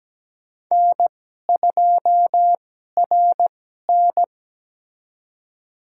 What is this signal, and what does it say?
Morse code, heard as a keyed tone, about 17 wpm, spaced as for 12 wpm, 700 Hz. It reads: N2RN
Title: USA Call Signs - 2x Character Spacing - 17wpm